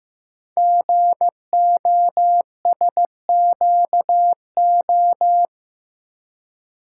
Morse code, heard as a keyed tone, about 15 wpm, 700 Hz. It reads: GOSQO
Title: Code Groups - 5 Letters - 15wpm